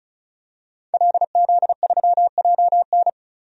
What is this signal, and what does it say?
Morse code, heard as a keyed tone, about 36 wpm, 700 Hz. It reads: LZ3JN
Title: DXCC-100 Call Signs - 5 Characters - 35wpm